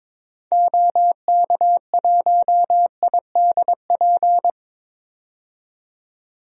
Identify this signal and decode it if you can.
Morse code, heard as a keyed tone, about 22 wpm, 700 Hz. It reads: OK1IDP
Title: DXCC-100 Call Signs - 6 Characters - 22wpm